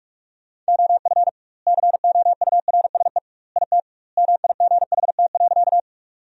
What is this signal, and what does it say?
Morse code, heard as a keyed tone, about 46 wpm, 700 Hz. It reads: OF COURSE IT MIGHT.
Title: 2x Word-Spacing - Sentences from Top 500 Words - 45wpm